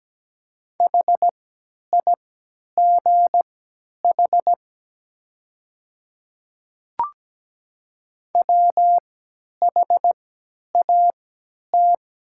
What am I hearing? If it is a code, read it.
Morse code, heard as a keyed tone, about 17 wpm, spaced as for 10 wpm, 700 Hz. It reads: HIGH WHAT
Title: 3x Character Spacing - Top 200 Words - 17wpm